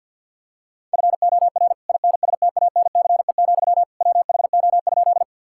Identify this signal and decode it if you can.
Morse code, heard as a keyed tone, about 51 wpm, 690 Hz. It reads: FOR INSTANCE, WHO?
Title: Sentences from Top 1000 Words - Review Part 5 - 50wpm